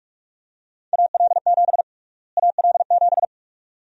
Morse code, heard as a keyed tone, about 46 wpm, 700 Hz. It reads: AL7 AL7
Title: USA Call Sign Prefixes - Repeated 2x - 45wpm